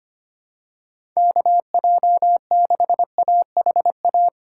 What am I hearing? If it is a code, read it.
Morse code, heard as a keyed tone, about 25 wpm, 700 Hz. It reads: KJ6AHA
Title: USA Call Signs - 6 Characters - 3x Character Spacing - 25wpm